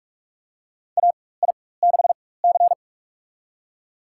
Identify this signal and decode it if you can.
Morse code, heard as a keyed tone, about 46 wpm, spaced as for 21 wpm, 700 Hz. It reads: AI6C
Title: USA Call Signs - 4 Characters - 4x Character Spacing - 45wpm